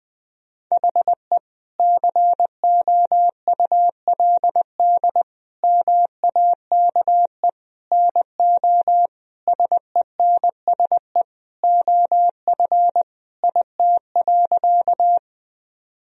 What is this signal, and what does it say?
Morse code, heard as a keyed tone, about 20 wpm, 700 Hz. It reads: HE COULD MAKE NO SENSE OF IT.
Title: Sentences from Top 1000 Words - Review Part 4 - 20wpm